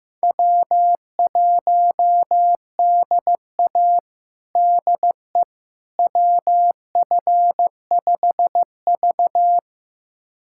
Morse code, heard as a keyed tone, about 15 wpm, 700 Hz. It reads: W1DA DE WF5V
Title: USA Callsign DE Callsign - 4 Characters - 15wpm